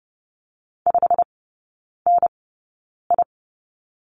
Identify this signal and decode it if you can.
Morse code, heard as a keyed tone, about 30 wpm, spaced as for 9 wpm, 700 Hz. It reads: 5NI
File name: Lesson 16e - Letter c - Three Characters 30wpm